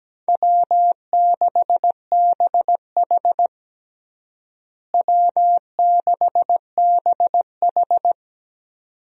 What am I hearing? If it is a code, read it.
Morse code, heard as a keyed tone, about 17 wpm, 700 Hz. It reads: W6BH W6BH
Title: USA Call Signs - 4 Characters - Repeated 2x - 17wpm